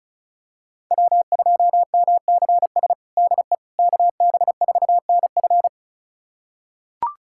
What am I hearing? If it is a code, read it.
Morse code, heard as a keyed tone, about 34 wpm, 700 Hz. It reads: W2MCS DE KB4NF
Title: USA Callsign DE Callsign - 5 Characters - 35wpm